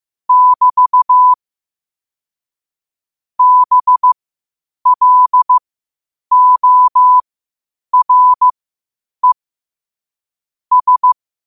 Morse code, heard as a keyed tone, about 15 wpm, spaced as for 8 wpm, 1000 Hz. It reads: <BT> BLORE S
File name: qotdfiles / 8wpm